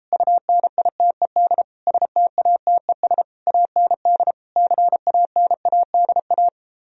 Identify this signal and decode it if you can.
Morse code, heard as a keyed tone, about 34 wpm, 700 Hz. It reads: UNITED STATES AND CANADA